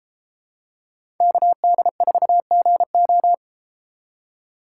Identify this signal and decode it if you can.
Morse code, heard as a keyed tone, about 32 wpm, 700 Hz. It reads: KD4GO